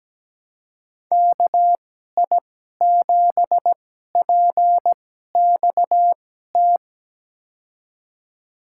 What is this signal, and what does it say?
Morse code, heard as a keyed tone, about 17 wpm, spaced as for 12 wpm, 700 Hz. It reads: KI7PXT
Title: USA Call Signs - 2x Character Spacing - 17wpm